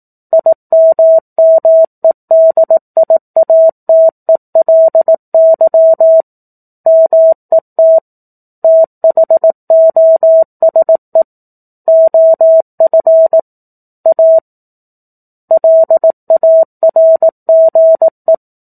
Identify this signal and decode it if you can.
Morse code, heard as a keyed tone, about 18 wpm, 650 Hz. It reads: IMMEDIATELY MET THOSE OF A LARGE
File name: 04-Chapter-04 18 WPM